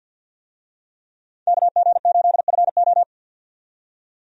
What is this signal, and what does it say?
Morse code, heard as a keyed tone, about 50 wpm, 700 Hz. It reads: KG8FO
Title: USA Call Signs - 5 Characters - 50wpm